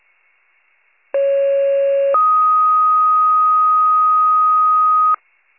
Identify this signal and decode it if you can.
Two-tone page, heard as a one-second tone, then a three-second tone, about 550 then 1250 Hz. It is heard from 1.1 to 5.2 s.